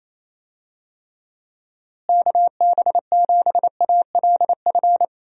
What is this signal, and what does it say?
Morse code, heard as a keyed tone, about 27 wpm, 700 Hz. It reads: KB7ALF
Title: USA Call Signs - 6 Characters - Repeated 3x - 28wpm